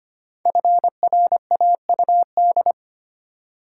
Morse code, heard as a keyed tone, about 25 wpm, 700 Hz. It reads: FRAUD
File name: 5 Letter Words - First 3 Letters Given - 25wpm